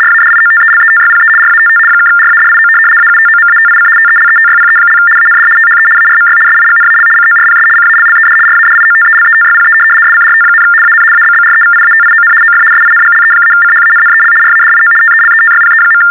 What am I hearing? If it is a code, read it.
RTTY, about 75 baud. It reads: RADIOTELETYPE RATT (RADIO AUTOMATIC TELETYPE) AND THE ARMY SIGNAL CORPS CALLED RADIOTELETYPE SCRT, AN ABBREVIATION OF SINGLE-CHANNEL RADIO TELETYPE.
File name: teletype-RTTY-75